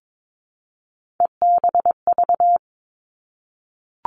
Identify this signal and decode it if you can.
Morse code, heard as a keyed tone, about 22 wpm, 700 Hz. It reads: EBV
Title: Three Challenging Letter-Number Combinations 22wpm